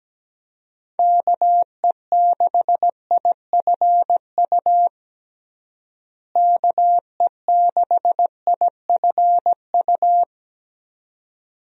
Morse code, heard as a keyed tone, about 17 wpm, 700 Hz. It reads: KE6IFU KE6IFU
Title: USA Call Signs - 6 Characters - Repeated 2x - 17wpm